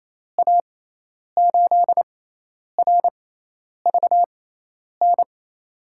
Morse code, heard as a keyed tone, about 28 wpm, spaced as for 10 wpm, 700 Hz. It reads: A8RVN
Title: USA Call Signs - 6x Character Spacing - 28wpm